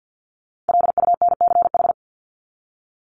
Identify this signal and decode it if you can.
Morse code, heard as a keyed tone, about 48 wpm, 700 Hz. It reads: LUNCH
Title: Top 1751-2000 Words 50wpm